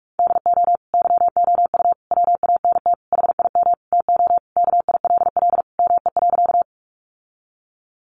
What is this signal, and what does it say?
Morse code, heard as a keyed tone, about 44 wpm, 700 Hz. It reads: DO YOU WANT HIM TO KILL ME.
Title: Sentences from Top 1400 Words 45wpm